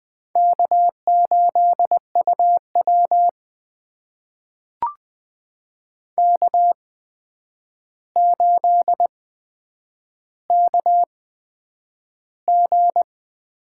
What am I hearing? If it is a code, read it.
Morse code, heard as a keyed tone, about 20 wpm, spaced as for 9 wpm, 700 Hz. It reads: K8UW K8KG
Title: USA Call Signs - 4 Characters - 8x Character Spacing - 20wpm